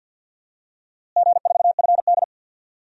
Morse code, heard as a keyed tone, about 46 wpm, 700 Hz. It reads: G4FD
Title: DXCC-100 Call Signs - 4 Characters - Repeated 3x - 50wpm